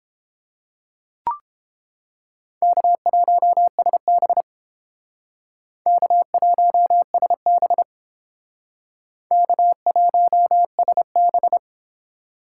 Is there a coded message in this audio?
Yes: Morse code from 2.6 to 11.6 s, heard as a keyed tone, about 31 wpm, 700 Hz. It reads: K1SB K1SB K1SB